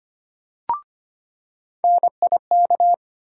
Morse code, heard as a keyed tone, about 25 wpm, 700 Hz. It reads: NIK
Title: USA Call Sign Postfixes - 3 Letters 25wpm